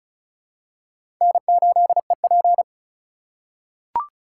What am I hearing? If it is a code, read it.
Morse code, heard as a keyed tone, about 34 wpm, 700 Hz. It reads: N8EP
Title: USA Call Signs - 4 Characters - 3x Character Spacing - 35wpm